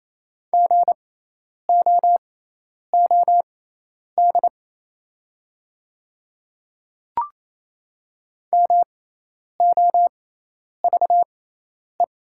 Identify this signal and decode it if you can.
Morse code, heard as a keyed tone, about 28 wpm, spaced as for 10 wpm, 700 Hz. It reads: GOOD MOVE
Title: 6x Character Spacing - Top 300 Words - 28wpm